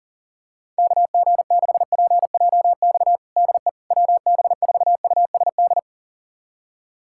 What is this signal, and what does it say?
Morse code, heard as a keyed tone, about 40 wpm, 700 Hz. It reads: KG6PJX DE WB4USD
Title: USA Callsign DE Callsign - 6 Characters - 40wpm